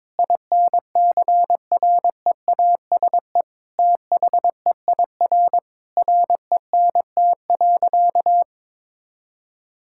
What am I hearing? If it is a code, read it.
Morse code, heard as a keyed tone, about 22 wpm, 700 Hz. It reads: INCREASE THEIR RENT.
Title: Sentences from Top 1000 Words - Review Part 5 - 22wpm